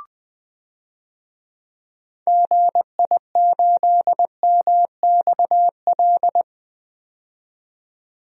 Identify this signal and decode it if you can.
Morse code, heard as a keyed tone, about 20 wpm, 700 Hz. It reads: GI8MXL
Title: DXCC-100 Call Signs - 6 Characters - 20wpm